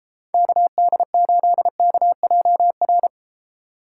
Morse code, heard as a keyed tone, about 33 wpm, 700 Hz. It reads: KD8KJR